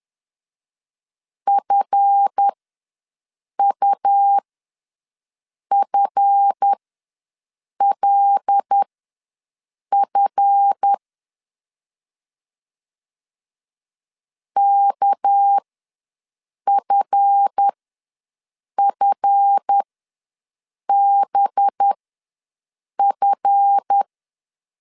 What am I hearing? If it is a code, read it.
Morse code, heard as a keyed tone, about 10 wpm, spaced as for 6 wpm, 800 Hz. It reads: FUFLF KFFBF
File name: LEZIONI / 7 GF9Z / 7 GF ws 10wpm 3918